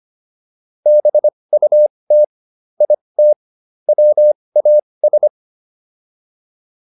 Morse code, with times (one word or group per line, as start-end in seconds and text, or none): 0.9-2.2 BUT
2.8-3.3 IT
3.9-5.3 WAS